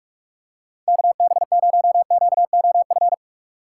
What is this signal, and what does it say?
Morse code, heard as a keyed tone, about 46 wpm, 700 Hz. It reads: KB0QOF
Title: USA Call Signs - 6x Character Spacing - 45wpm